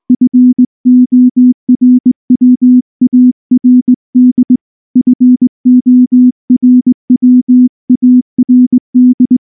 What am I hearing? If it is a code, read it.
Morse code, heard as a keyed tone, about 19 wpm, 250 Hz. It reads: FORWARD FORWARD
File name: morse / morse-training-20WPM-5